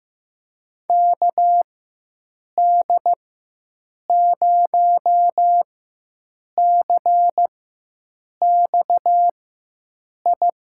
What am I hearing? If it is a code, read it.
Morse code, heard as a keyed tone, about 15 wpm, spaced as for 7 wpm, 700 Hz. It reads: KD0CXI
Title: USA Call Signs - 4x Character Spacing - 15wpm